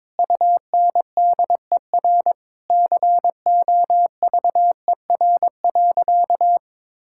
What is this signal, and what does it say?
Morse code, heard as a keyed tone, about 22 wpm, 700 Hz. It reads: UNDER COVER.
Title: Sentences from Top 1000 Words - Review Part 6 - 22wpm